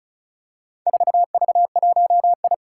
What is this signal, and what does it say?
Morse code, heard as a keyed tone, about 36 wpm, 700 Hz. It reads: 4V1I
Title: Four Challenging Letter-Number Combinations 35wpm